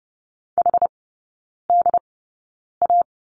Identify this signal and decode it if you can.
Morse code, heard as a keyed tone, about 30 wpm, spaced as for 9 wpm, 700 Hz. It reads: HDA
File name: Lesson 18e - Letter m - Three Characters 30wpm